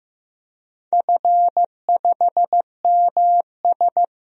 Morse code, heard as a keyed tone, about 15 wpm, 700 Hz. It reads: F5MS
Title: DXCC-100 Call Signs - 4 Characters - Repeated 3x - 15wpm